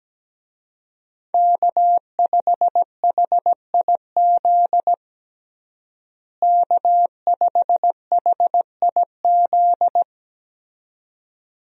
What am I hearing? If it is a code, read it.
Morse code, heard as a keyed tone, about 17 wpm, 700 Hz. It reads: K5HIZ K5HIZ
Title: USA Call Signs - 5 Characters - Repeated 2x - 17wpm